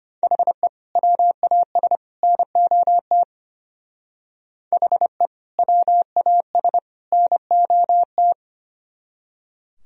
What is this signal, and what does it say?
Morse code, heard as a keyed tone, about 33 wpm, 700 Hz. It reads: HE WAS NOT HE WAS NOT